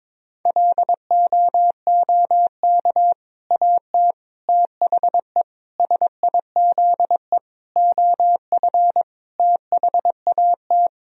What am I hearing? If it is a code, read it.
Morse code, heard as a keyed tone, about 22 wpm, 700 Hz. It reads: LOOK AT THE SIZE OF THAT
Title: Sentences from Top 1000 Words - Review Part 3 - 22wpm